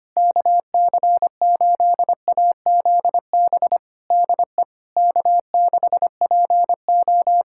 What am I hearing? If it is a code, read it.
Morse code, heard as a keyed tone, about 25 wpm, 700 Hz. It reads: KC8AZB DE K6PO
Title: Practice QSO - Exchange 4 - Ending - 25wpm